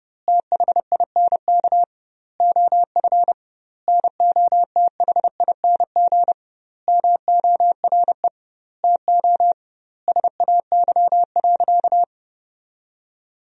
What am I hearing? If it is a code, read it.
Morse code, heard as a keyed tone, about 30 wpm, 700 Hz. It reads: THINK OF NOTHING MORE TO SAY.
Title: 2x Word-Spacing - Sentences from Top 500 Words - 30wpm